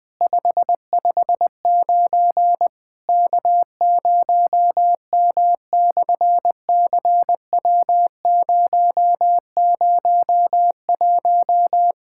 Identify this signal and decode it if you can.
Morse code, heard as a keyed tone, about 20 wpm, 700 Hz. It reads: 559 K0M/CW001
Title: SOTA QSO Practice 20wpm